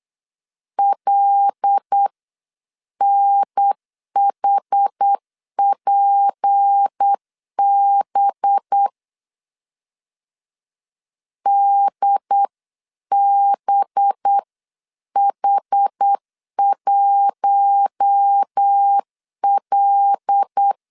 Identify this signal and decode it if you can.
Morse code, heard as a keyed tone, about 8 wpm, 800 Hz. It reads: LNHPB DBH1L
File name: 8wpm